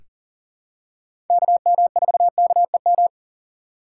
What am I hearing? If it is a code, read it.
Morse code, heard as a keyed tone, about 40 wpm, 700 Hz. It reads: KM4KEM
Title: USA Call Signs - 6 Characters - 6x Character Spacing - 40wpm